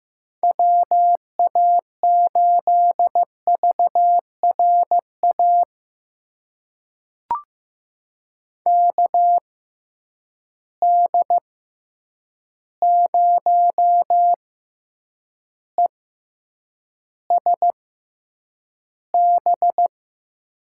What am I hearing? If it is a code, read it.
Morse code, heard as a keyed tone, about 15 wpm, spaced as for 8 wpm, 700 Hz. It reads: WA8VRA KD0ESB
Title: USA Call Signs - 6 Characters - 6x Character Spacing - 15wpm